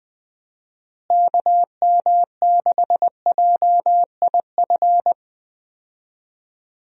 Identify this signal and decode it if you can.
Morse code, heard as a keyed tone, about 20 wpm, 700 Hz. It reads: KM6JIF
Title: USA Call Signs - 6 Characters - Repeated 2x - 20wpm